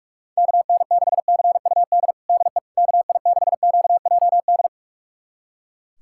Morse code, heard as a keyed tone, about 42 wpm, 690 Hz. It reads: KN6CUD DE KI6QJD
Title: USA Callsign DE Callsign - 6 Characters - 45wpm